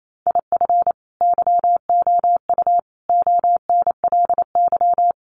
Morse code, heard as a keyed tone, about 28 wpm, 700 Hz. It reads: IF YOU ONLY
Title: Sentences from Top 400 Words 28wpm